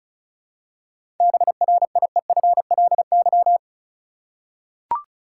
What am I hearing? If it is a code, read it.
Morse code, heard as a keyed tone, about 34 wpm, 700 Hz. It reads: BRIEFLY
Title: Seven Letter Words - 35wpm